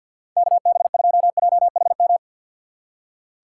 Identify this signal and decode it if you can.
Morse code, heard as a keyed tone, about 53 wpm, 700 Hz. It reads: KB2JHM
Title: USA Call Signs - 4x Character Spacing - 50wpm